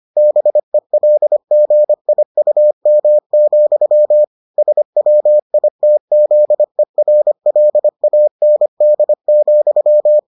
Morse code, heard as a keyed tone, about 25 wpm, 600 Hz. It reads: BELGIUM, SWITZERLAND,